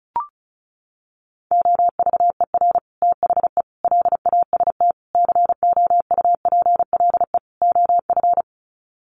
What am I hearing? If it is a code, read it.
Morse code, heard as a keyed tone, about 34 wpm, 700 Hz. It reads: OVER THE LAST COUPLE OF
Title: Sets of 5 Words Encore 35wpm